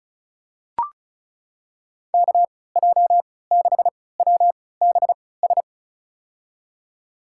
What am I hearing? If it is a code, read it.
Morse code, heard as a keyed tone, about 36 wpm, spaced as for 20 wpm, 700 Hz. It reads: KJ6WBS
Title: USA Call Signs - 6 Characters - 3x Character Spacing - 35wpm